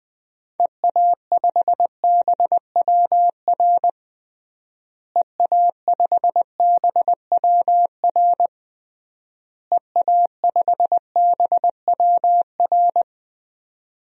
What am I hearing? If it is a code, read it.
Morse code, heard as a keyed tone, about 20 wpm, 700 Hz. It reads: EA5BWR EA5BWR EA5BWR
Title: DXCC-100 Call Signs - 6 Characters - Repeated 3x - 20wpm